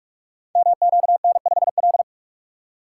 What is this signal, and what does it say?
Morse code, heard as a keyed tone, about 44 wpm, 700 Hz. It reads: MQN5L
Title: Code Groups - 5 Letter:Numbers - 45wpm